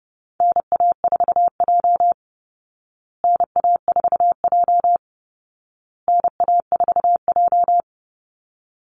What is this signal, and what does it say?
Morse code, heard as a keyed tone, about 30 wpm, 700 Hz. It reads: NA4J NA4J NA4J
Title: Call Signs Repeated 2x - 30wpm